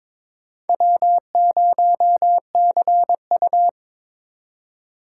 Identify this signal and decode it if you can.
Morse code, heard as a keyed tone, about 22 wpm, 700 Hz. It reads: W0CU
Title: USA Call Signs - 4x Character Spacing - 22wpm